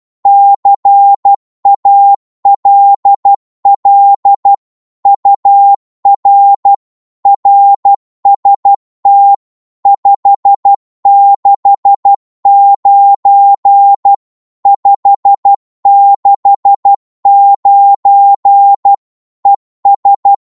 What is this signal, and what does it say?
Morse code, heard as a keyed tone, about 12 wpm, 800 Hz. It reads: CALL UR RST 569 569 ES
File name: Morse / qso-12wpm